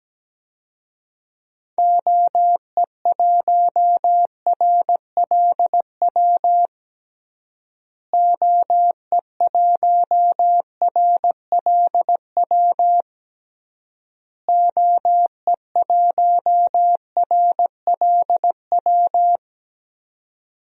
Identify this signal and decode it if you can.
Morse code, heard as a keyed tone, about 17 wpm, 700 Hz. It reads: OE1RLW OE1RLW OE1RLW
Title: DXCC-100 Call Signs - 6 Characters - Repeated 3x - 17wpm